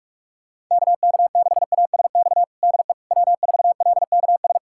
Morse code, heard as a keyed tone, about 41 wpm, 700 Hz. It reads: KK6ASX DE W4LKS
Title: USA Callsign DE Callsign 45wpm